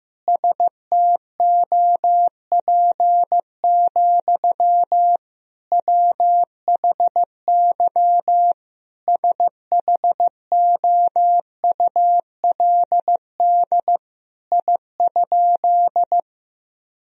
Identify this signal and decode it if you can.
Morse code, heard as a keyed tone, about 15 wpm, 700 Hz. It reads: STOP, WHY SHOULD I?
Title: Sentences from Top 1000 Words - Review Part 1 - 15wpm